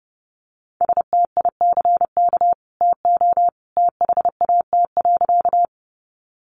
Morse code, heard as a keyed tone, about 30 wpm, 700 Hz. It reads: STICK TO THAT.
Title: Sentences from Top 1200 Words 30wpm